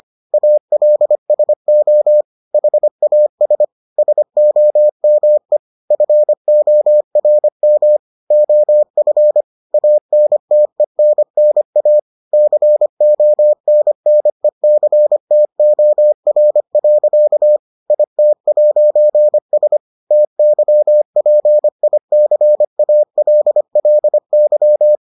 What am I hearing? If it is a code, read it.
Morse code, heard as a keyed tone, about 25 wpm, 600 Hz. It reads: ALSO HAS SOME FORM OF ANTENNA CONNECTOR. IT'S TYPICALLY